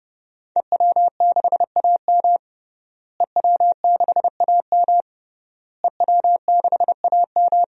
Morse code, heard as a keyed tone, about 30 wpm, 700 Hz. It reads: EW6AM EW6AM EW6AM
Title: DXCC-100 Call Signs - 5 Characters - Repeated 3x - 30wpm